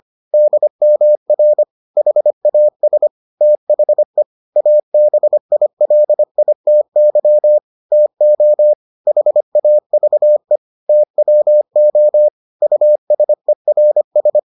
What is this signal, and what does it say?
Morse code, heard as a keyed tone, about 25 wpm, 600 Hz. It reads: DMR HAS THE ABILITY TO HAVE TWO USERS